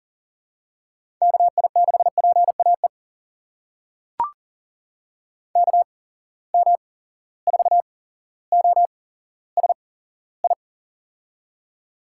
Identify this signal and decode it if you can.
Morse code, heard as a keyed tone, about 40 wpm, spaced as for 17 wpm, 700 Hz. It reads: KI6PAE KM4OSI
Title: USA Call Signs - 6 Characters - 8x Character Spacing - 40wpm